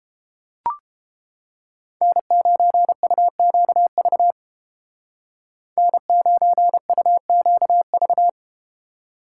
Morse code, heard as a keyed tone, about 31 wpm, 700 Hz. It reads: N9UQV N9UQV